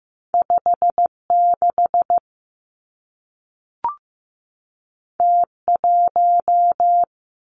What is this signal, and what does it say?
Morse code, heard as a keyed tone, about 15 wpm, 700 Hz. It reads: 5 6 T 1
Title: Two Challenging Letter-Number Combinations 15wpm